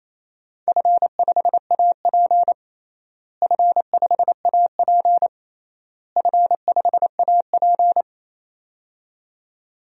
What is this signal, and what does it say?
Morse code, heard as a keyed tone, about 28 wpm, 700 Hz. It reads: F5AP F5AP F5AP
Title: DXCC-100 Call Signs - 4 Characters - Repeated 3x - 28wpm